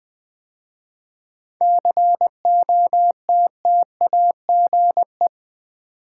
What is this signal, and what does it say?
Morse code, heard as a keyed tone, about 20 wpm, 700 Hz. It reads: COTTAGE
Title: 7 Letter Words - First 3 Letters Given - 20wpm